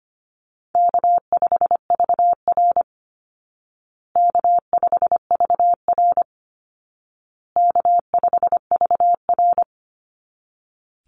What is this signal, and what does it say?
Morse code, heard as a keyed tone, about 25 wpm, 700 Hz. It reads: K5VR K5VR K5VR
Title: Call Signs Repeated 2x - 25wpm